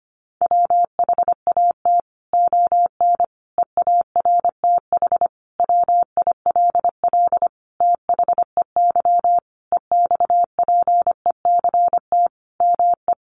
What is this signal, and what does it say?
Morse code, heard as a keyed tone, about 25 wpm, 700 Hz. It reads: WHAT ON EARTH WILL THEY EXPECT ME
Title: Sentences from Top 1100 Words 25wpm